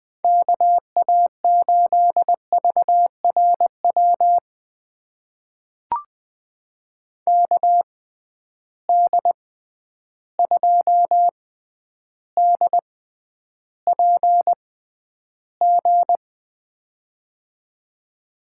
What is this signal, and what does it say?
Morse code, heard as a keyed tone, about 20 wpm, spaced as for 10 wpm, 700 Hz. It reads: KA8VRW KD2DPG
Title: USA Call Signs - 6 Characters - 6x Character Spacing - 20wpm